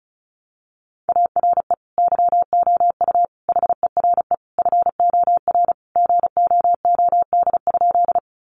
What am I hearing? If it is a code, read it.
Morse code, heard as a keyed tone, about 34 wpm, 700 Hz. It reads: ARE YOU HERE FOR GOOD?
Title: Sentences from Top 200 Words 35wpm